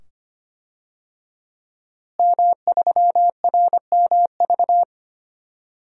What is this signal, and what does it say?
Morse code, heard as a keyed tone, about 25 wpm, 700 Hz. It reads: M3RMV
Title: DXCC-100 Call Signs - 5 Characters - Repeated 2x - 25wpm